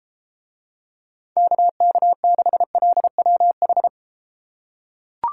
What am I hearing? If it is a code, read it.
Morse code, heard as a keyed tone, about 32 wpm, 700 Hz. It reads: KK6LWH